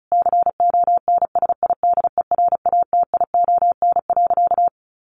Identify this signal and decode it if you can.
Morse code, heard as a keyed tone, about 34 wpm, 700 Hz. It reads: CONSIDERATION.
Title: Sentences from Top 1500 Words 35wpm